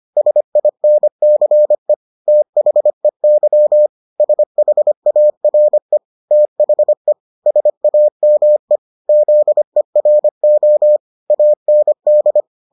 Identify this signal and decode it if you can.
Morse code, heard as a keyed tone, about 25 wpm, 600 Hz. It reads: SINCE THEY SHARE THE SAME ZERO AND